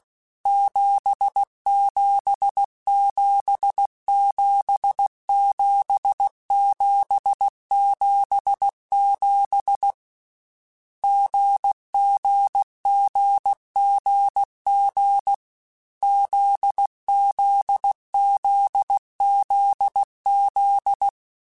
Morse code, heard as a keyed tone, about 16 wpm, 800 Hz. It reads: 77777777 GGGGG ZZZZZ
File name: Short Runs / Short Runs-mp3s / GZ GZ7 @15